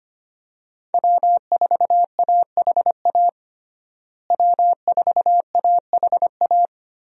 Morse code, heard as a keyed tone, about 25 wpm, 700 Hz. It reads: W4AHA W4AHA
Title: USA Call Signs - 5 Characters - Repeated 3x - 25wpm